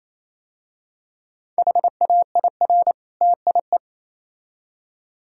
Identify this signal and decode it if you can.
Morse code, heard as a keyed tone, about 28 wpm, 700 Hz. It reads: HAIR TIE
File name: Easy Words - Speed-Racing with Hints - 22wpm